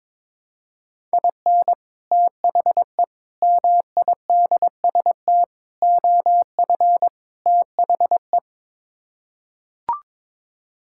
Morse code, heard as a keyed tone, about 21 wpm, 700 Hz. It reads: IN THE MIDST OF THE
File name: Build A Phrase - 5-Word Phrases - 22wpm